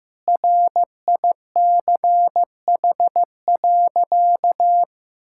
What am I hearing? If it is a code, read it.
Morse code, heard as a keyed tone, about 16 wpm, 700 Hz. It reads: RICH.
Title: Sentences from Top 1600 Words 15wpm